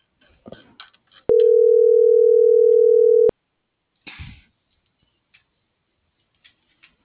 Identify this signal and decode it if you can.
Ringback tone (North America), 440 + 480 Hz.